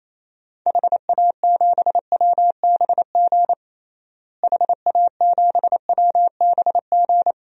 Morse code, heard as a keyed tone, about 28 wpm, 700 Hz. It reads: HA7WBG HA7WBG
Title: DXCC-100 Call Signs - 6 Characters - Repeated 2x - 28wpm